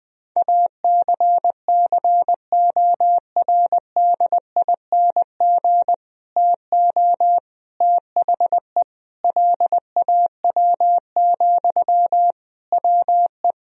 Morse code, heard as a keyed tone, about 20 wpm, 700 Hz. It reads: ACCORDING TO THE LAW, WE
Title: Sentences from Top 1000 Words - Review Part 4 - 20wpm